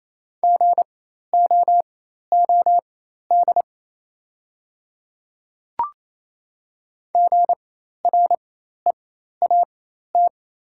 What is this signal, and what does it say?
Morse code, heard as a keyed tone, about 28 wpm, spaced as for 13 wpm, 700 Hz. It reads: GOOD GREAT
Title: 4x Character Spacing - Top 300 Words - 28wpm